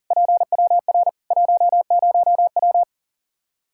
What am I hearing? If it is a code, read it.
Morse code, heard as a keyed tone, about 37 wpm, 700 Hz. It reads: PWR 10W
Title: Practice QSO - Exchange 2 - Info - 40wpm